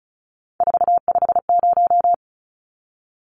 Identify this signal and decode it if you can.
Morse code, heard as a keyed tone, about 34 wpm, 700 Hz. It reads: 450
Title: Numbers - 3-Digit 35wpm